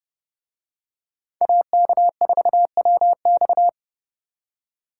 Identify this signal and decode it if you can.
Morse code, heard as a keyed tone, about 29 wpm, 700 Hz. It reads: AK4WX